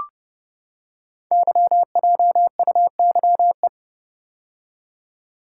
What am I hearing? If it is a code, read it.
Morse code, heard as a keyed tone, about 30 wpm, 690 Hz. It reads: YJUYE
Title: Code Groups - 5 Letters - 30wpm